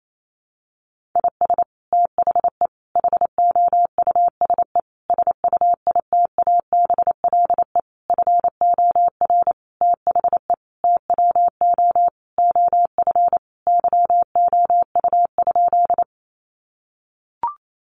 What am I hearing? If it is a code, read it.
Morse code, heard as a keyed tone, about 28 wpm, 700 Hz. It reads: IS THE HOUSE SUITABLE FOR THE TWO OF YOU?
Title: Sentences from Top 1400 Words 28wpm